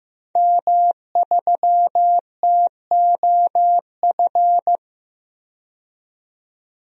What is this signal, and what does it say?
Morse code, heard as a keyed tone, about 15 wpm, 700 Hz. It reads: M3TOF
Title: DXCC-100 Call Signs - 5 Characters - 15wpm